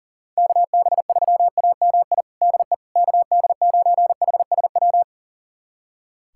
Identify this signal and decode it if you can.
Morse code, heard as a keyed tone, about 40 wpm, 700 Hz. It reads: KB3AMI DE KD9HSW
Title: USA Callsign DE Callsign - 6 Characters - 40wpm